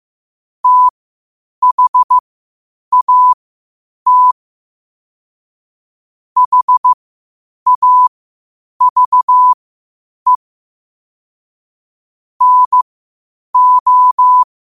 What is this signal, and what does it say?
Morse code, heard as a keyed tone, about 15 wpm, spaced as for 8 wpm, 1000 Hz. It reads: THAT HAVE NO
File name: qotdfiles / 8wpm